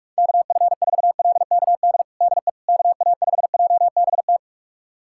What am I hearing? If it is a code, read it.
Morse code, heard as a keyed tone, about 46 wpm, 700 Hz. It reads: KF4LKD DE KA5JBT
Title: USA Callsign DE Callsign - 6 Characters - 45wpm